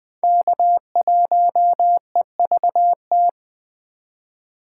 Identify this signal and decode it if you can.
Morse code, heard as a keyed tone, about 20 wpm, 700 Hz. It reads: K1EVT
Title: Speed Racing - USA 1x3 Call Signs 17wpm